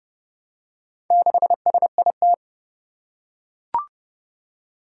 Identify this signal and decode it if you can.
Morse code, heard as a keyed tone, about 29 wpm, 700 Hz. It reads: 6SIT
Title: Four Challenging Letter-Number Combinations 30wpm